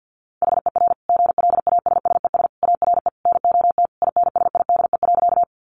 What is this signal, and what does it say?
Morse code, heard as a keyed tone, about 50 wpm, 700 Hz. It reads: HER GLASSES ARE NOT INSIDE.
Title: Additional Easy Phrases 50wpm